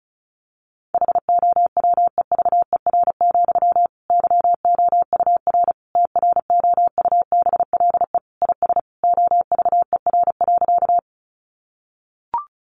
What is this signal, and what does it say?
Morse code, heard as a keyed tone, about 34 wpm, 700 Hz. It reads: HOWEVER, YOUR TROUBLE IS OVER.
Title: Sentences from Top 1200 Words 35wpm